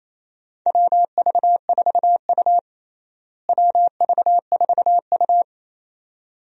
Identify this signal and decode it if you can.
Morse code, heard as a keyed tone, about 28 wpm, 700 Hz. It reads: WV4U WV4U
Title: USA Call Signs - 4 Characters - Repeated 2x - 28wpm